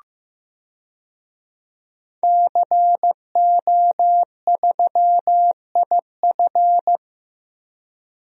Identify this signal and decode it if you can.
Morse code, heard as a keyed tone, about 15 wpm, 700 Hz. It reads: CO3IF
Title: DXCC-100 Call Signs - 5 Characters - 15wpm